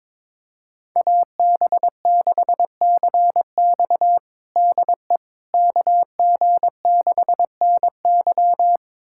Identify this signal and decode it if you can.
Morse code, heard as a keyed tone, about 22 wpm, 700 Hz. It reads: AB6CX DE KG6NY
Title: USA Callsign DE Callsign - 5 Characters - 22wpm